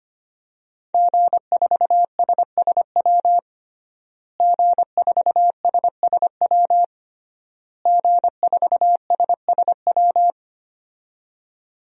Morse code, heard as a keyed tone, about 25 wpm, 700 Hz. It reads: G4SSW G4SSW G4SSW
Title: DXCC-100 Call Signs - 5 Characters - Repeated 3x - 25wpm